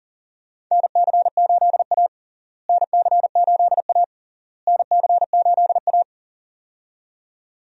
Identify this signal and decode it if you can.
Morse code, heard as a keyed tone, about 37 wpm, 700 Hz. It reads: NC8A NC8A NC8A
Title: USA Call Signs - 4 Characters - Repeated 3x - 40wpm